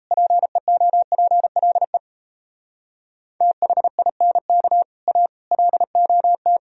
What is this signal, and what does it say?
Morse code, heard as a keyed tone, about 37 wpm, 700 Hz. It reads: PEOPLE THINK A LOT